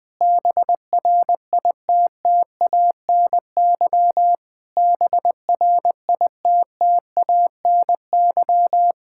Morse code, heard as a keyed tone, about 21 wpm, 700 Hz. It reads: BRITTANY BRITTANY
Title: Practice QSO - Exchange 1 - Beginning - 20wpm